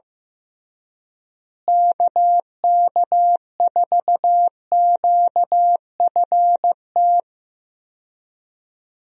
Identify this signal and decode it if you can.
Morse code, heard as a keyed tone, about 15 wpm, 700 Hz. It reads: KK4QFT
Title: USA Call Signs - 6 Characters - Repeated 3x - 15wpm